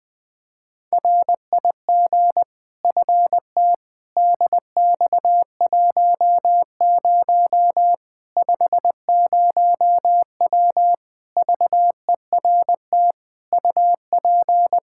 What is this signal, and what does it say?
Morse code, heard as a keyed tone, about 20 wpm, 700 Hz. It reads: RIG FT DX10 50W VERT UP
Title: Practice QSO - Exchange 2 - Info - 20wpm